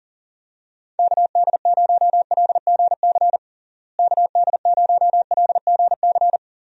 Morse code, heard as a keyed tone, about 40 wpm, 700 Hz. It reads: KD0LGC KD0LGC
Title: USA Call Signs - 6 Characters - Repeated 2x - 40wpm